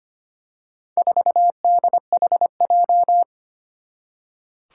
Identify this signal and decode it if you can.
Morse code, heard as a keyed tone, about 25 wpm, 700 Hz. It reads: 4DHJ
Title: Four Challenging Letter-Number Combinations 25wpm